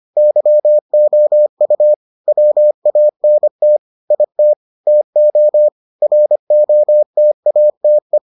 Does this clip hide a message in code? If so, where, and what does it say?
Yes: Morse code from 0.2 to 8.2 s, heard as a keyed tone, about 25 wpm, 600 Hz. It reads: YOU WANT IT TO ROTATE